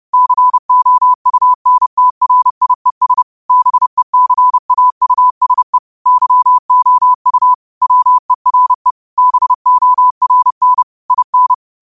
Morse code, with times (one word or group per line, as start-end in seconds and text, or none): 0.1-3.2 COUNTRIES
3.5-5.8 BECAUSE
6.1-7.5 YOU
7.8-8.9 WERE
9.2-10.8 BORN
11.1-11.5 IN